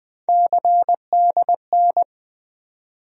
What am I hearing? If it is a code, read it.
Morse code, heard as a keyed tone, about 21 wpm, 700 Hz. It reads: CDN
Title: USA Call Sign Postfixes - 3 Letters 20wpm